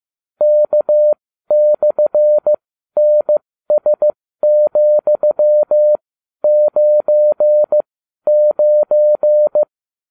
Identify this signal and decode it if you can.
Morse code, heard as a keyed tone, about 15 wpm, spaced as for 12 wpm, 600 Hz. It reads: K/NS,99
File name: cw / ubacw32 2